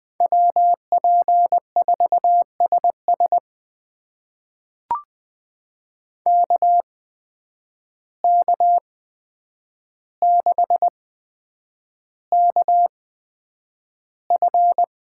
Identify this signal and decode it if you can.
Morse code, heard as a keyed tone, about 20 wpm, spaced as for 9 wpm, 700 Hz. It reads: WP4SS KK6KF
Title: USA Call Signs - 5 Characters - 8x Character Spacing - 20wpm